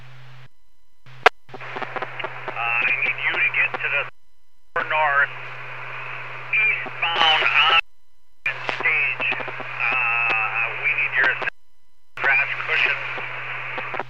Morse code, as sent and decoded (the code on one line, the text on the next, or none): none